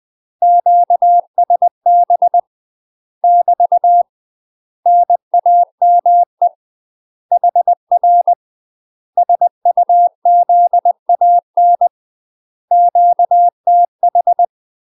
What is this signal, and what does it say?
Morse code, heard as a keyed tone, about 20 wpm, 700 Hz. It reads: QSB <BT> NAME HR SUZAN QTH
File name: UK MASTER QSOs TRAINING / 20WPM / UK MASTER QSOs 0018